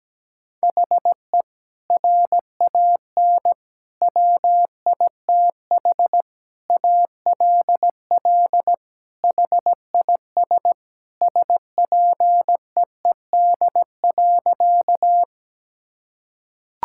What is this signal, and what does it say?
Morse code, heard as a keyed tone, about 17 wpm, 700 Hz. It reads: HE RAN WITH ALL HIS SPEED.
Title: Sentences from Top 1000 Words - Review Part 5 - 17wpm